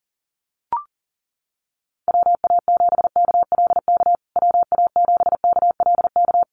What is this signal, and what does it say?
Morse code, heard as a keyed tone, about 40 wpm, 700 Hz. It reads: WA7KLK WA7KLK
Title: USA Call Signs Repeated 3x 40wpm